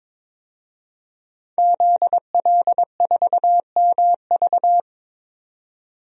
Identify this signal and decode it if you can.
Morse code, heard as a keyed tone, about 22 wpm, 700 Hz. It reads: ZL4MV
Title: DXCC-100 Call Signs - 5 Characters - 22wpm